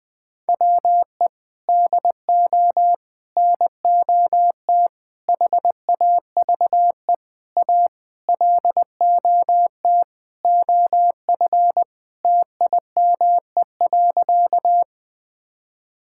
Morse code, heard as a keyed tone, about 20 wpm, 700 Hz. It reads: WE DO NOT HAVE A LOT OF TIME.
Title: Sentences with 5-Word Phrases - 20wpm